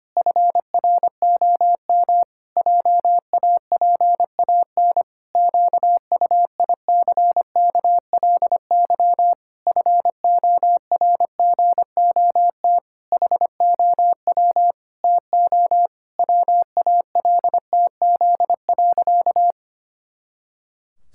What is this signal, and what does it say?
Morse code, heard as a keyed tone, about 25 wpm, 700 Hz. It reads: FROM JAPAN QUICKLY FORGOT HOW TO WALTZ.
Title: Pangram Challenge v2 - 25wpm